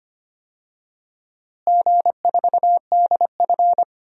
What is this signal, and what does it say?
Morse code, heard as a keyed tone, about 25 wpm, 700 Hz. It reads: G4DF
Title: DXCC-100 Call Signs - 4 Characters - Repeated 3x - 25wpm